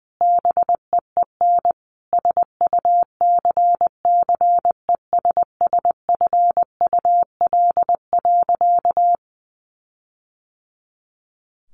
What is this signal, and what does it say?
Morse code, heard as a keyed tone, about 20 wpm, 700 Hz. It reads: BEEN SUCCESSFUL.
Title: Sentences from Top 1400 Words 20wpm